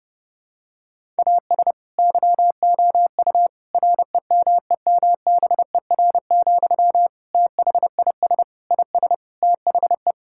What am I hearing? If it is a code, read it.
Morse code, heard as a keyed tone, about 30 wpm, 700 Hz. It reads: AS YOU REMEMBER, THIS IS THE
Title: Sentences from Top 1000 Words - Review Part 5 - 30wpm